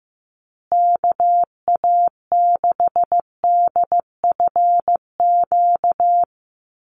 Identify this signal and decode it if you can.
Morse code, heard as a keyed tone, about 15 wpm, 700 Hz. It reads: KA6DFQ
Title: USA Call Signs Repeated 2x 15wpm